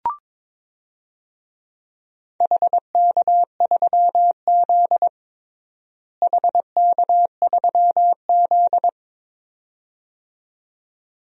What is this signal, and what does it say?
Morse code, heard as a keyed tone, about 22 wpm, 700 Hz. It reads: HK3Z HK3Z
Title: DXCC-100 Call Signs - 4 Characters - Repeated 2x - 22wpm